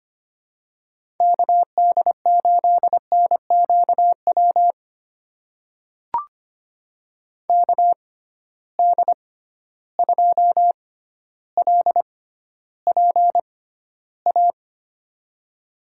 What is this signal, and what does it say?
Morse code, heard as a keyed tone, about 25 wpm, spaced as for 13 wpm, 700 Hz. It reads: KD8NQW KD2LPA
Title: USA Call Signs - 6 Characters - 6x Character Spacing - 25wpm